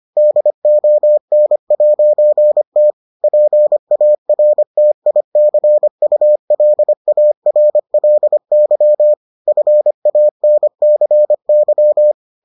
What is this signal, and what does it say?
Morse code, heard as a keyed tone, about 25 wpm, 600 Hz. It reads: DON'T PARTICULARLY FANCY